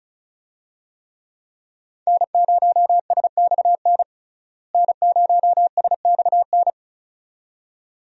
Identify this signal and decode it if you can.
Morse code, heard as a keyed tone, about 36 wpm, 700 Hz. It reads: N0SXN N0SXN
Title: USA Call Signs - 5 Characters - Repeated 2x - 35wpm